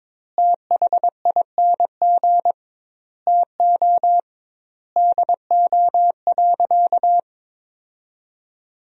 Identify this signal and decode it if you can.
Morse code, heard as a keyed tone, about 22 wpm, 700 Hz. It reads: THING TO DO.
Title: 2x Word-Spacing - Sentences from Top 100 Words - 22wpm